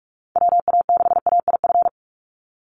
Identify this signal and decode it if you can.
Morse code, heard as a keyed tone, about 44 wpm, 690 Hz. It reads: WA6AIF
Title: USA Call Signs 45wpm